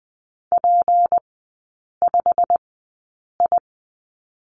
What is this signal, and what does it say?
Morse code, heard as a keyed tone, about 20 wpm, spaced as for 8 wpm, 690 Hz. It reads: P5I
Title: 159 Lesson 26e - Letter g - Three Characters 20wpm